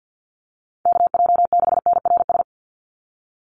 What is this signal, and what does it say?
Morse code, heard as a keyed tone, about 50 wpm, 700 Hz. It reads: KJ6NRS
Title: USA Call Signs 50wpm